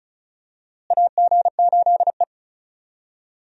Morse code, heard as a keyed tone, about 36 wpm, 700 Hz. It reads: AG8E
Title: USA Call Signs - 4 Characters - Repeated 2x - 35wpm